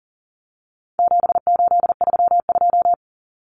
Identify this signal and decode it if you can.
Morse code, heard as a keyed tone, about 40 wpm, 700 Hz. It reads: 7832
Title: Common Cut Numbers - 4-Digit 40wpm